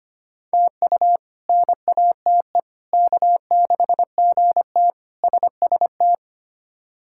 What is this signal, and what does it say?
Morse code, heard as a keyed tone, about 25 wpm, 700 Hz. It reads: TU NATE K6GT SST
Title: K1USN SST QSO Practice - 25wpm